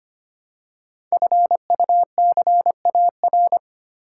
Morse code, heard as a keyed tone, about 25 wpm, 700 Hz. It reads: FUCAR
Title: Code Groups - 5 Letter:Numbers - 25wpm